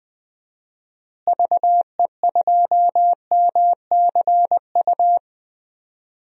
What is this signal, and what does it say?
Morse code, heard as a keyed tone, about 20 wpm, 700 Hz. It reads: VE2MCU
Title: DXCC-100 Call Signs - 6 Characters - Repeated 3x - 20wpm